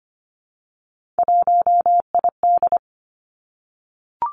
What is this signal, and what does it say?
Morse code, heard as a keyed tone, about 25 wpm, 700 Hz. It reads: 1ID
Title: Three Challenging Letter-Number Combinations 25wpm